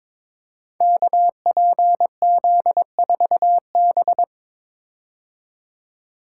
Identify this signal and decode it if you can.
Morse code, heard as a keyed tone, about 21 wpm, 700 Hz. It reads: KPZ4B
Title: Code Groups - 5 Letter:Numbers - 22wpm